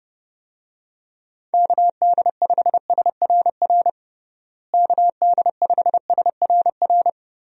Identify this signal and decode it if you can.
Morse code, heard as a keyed tone, about 32 wpm, 700 Hz. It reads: KD5SRR KD5SRR
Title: USA Call Signs - 6 Characters - Repeated 3x - 30wpm